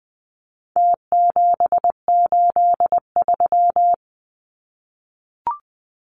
Morse code, heard as a keyed tone, about 20 wpm, 700 Hz. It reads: T783
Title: Numbers - Contest Style - 4-Digit 20wpm